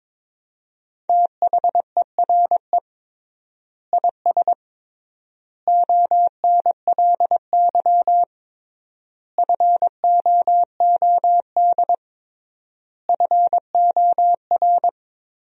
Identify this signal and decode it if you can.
Morse code, heard as a keyed tone, about 22 wpm, 700 Hz. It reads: THERE IS ONLY FOOD FOR
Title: 3x Word-Spacing - Sentences from Top 300 Words - 22wpm